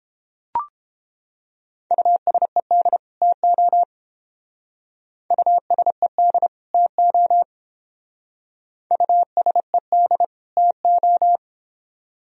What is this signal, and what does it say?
Morse code, heard as a keyed tone, about 28 wpm, 700 Hz. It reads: USED TO USED TO USED TO